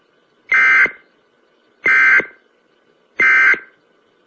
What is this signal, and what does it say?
Emergency Alert System (SAME) tones, heard as harsh buzzing bursts. It is heard from 0.5 to 3.7 s.